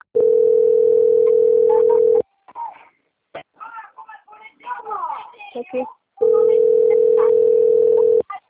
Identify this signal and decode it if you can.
Ringback tone (North America), 440 + 480 Hz, with speech.
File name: out- - - - - .10030